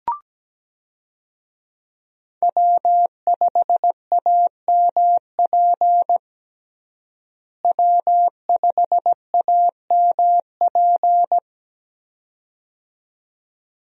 Morse code, heard as a keyed tone, about 17 wpm, 700 Hz. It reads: W5AMP W5AMP
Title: USA Call Signs - 5 Characters - Repeated 2x - 17wpm